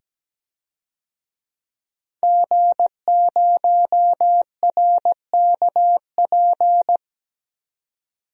Morse code, heard as a keyed tone, about 17 wpm, 700 Hz. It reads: G0RKP
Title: DXCC-100 Call Signs - 5 Characters - 17wpm